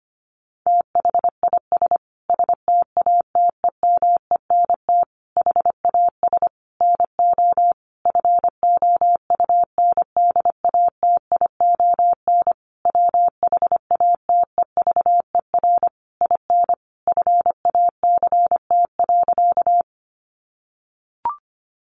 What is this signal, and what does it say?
Morse code, heard as a keyed tone, about 25 wpm, 700 Hz. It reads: THIS STATEMENT HAS NO FOUNDATION WHATEVER IN FACT.
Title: Sentences from Top 1200 Words 25wpm